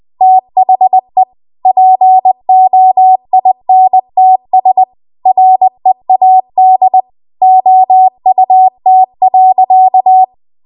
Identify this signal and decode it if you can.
Morse code, heard as a keyed tone, about 20 wpm, 750 Hz. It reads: THE POINTS READ OUT.
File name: Archive / 20 WPM / 20WPM